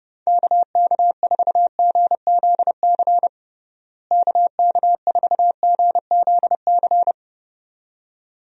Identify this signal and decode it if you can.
Morse code, heard as a keyed tone, about 30 wpm, 700 Hz. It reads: KK4GZC KK4GZC
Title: USA Call Signs - 6 Characters - Repeated 2x - 30wpm